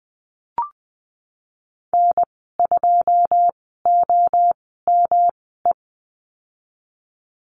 Morse code, heard as a keyed tone, about 20 wpm, spaced as for 14 wpm, 700 Hz. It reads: N2OME
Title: 154 Lesson 25g - Letter p - Call Signs 20wpm